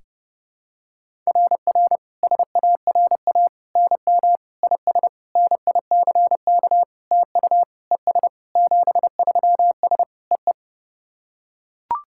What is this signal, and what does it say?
Morse code, heard as a keyed tone, about 32 wpm, 700 Hz. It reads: RR SARA NM IS NICK TU ES 73S EE
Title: SOTA QSO Practice 30wpm